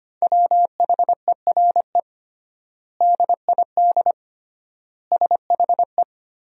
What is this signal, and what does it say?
Morse code, heard as a keyed tone, about 25 wpm, 700 Hz. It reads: WHERE DID SHE
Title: 3x Word-Spacing - Sentences from Top 200 Words - 25wpm